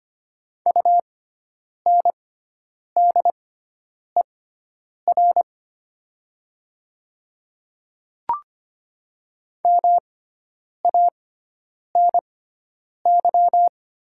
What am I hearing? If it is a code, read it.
Morse code, heard as a keyed tone, about 25 wpm, spaced as for 9 wpm, 700 Hz. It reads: UNDER MANY
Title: 6x Character Spacing - Top 200 Words - 25wpm